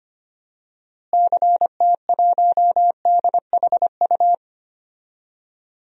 Morse code, heard as a keyed tone, about 25 wpm, 700 Hz. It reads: CT1DHU
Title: DXCC-100 Call Signs - 6 Characters - Repeated 3x - 25wpm